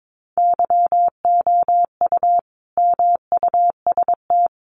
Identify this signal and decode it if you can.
Morse code, heard as a keyed tone, about 22 wpm, 700 Hz. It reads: YOU MUST
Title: Sentences from Top 400 Words 22wpm